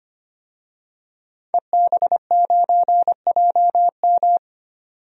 Morse code, heard as a keyed tone, about 25 wpm, 700 Hz. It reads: EB9JM
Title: DXCC-100 Call Signs - 5 Characters - 25wpm